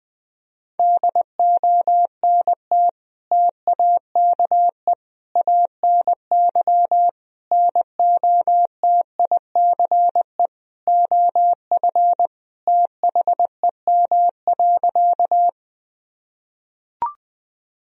Morse code, heard as a keyed tone, about 21 wpm, 700 Hz. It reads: DONT TAKE ANY NOTICE OF THEM.
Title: Sentences from Top 1000 Words - Review Part 2 - 20wpm